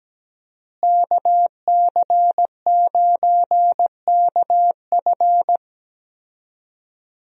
Morse code, heard as a keyed tone, about 17 wpm, 700 Hz. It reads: KC9KF
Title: USA Call Signs - 5 Characters - Repeated 2x - 17wpm